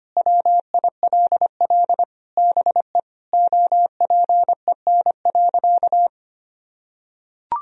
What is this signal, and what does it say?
Morse code, heard as a keyed tone, about 25 wpm, 700 Hz. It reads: WILL BE OPEN.
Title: Sentences from Top 1000 Words - Review Part 1 - 25wpm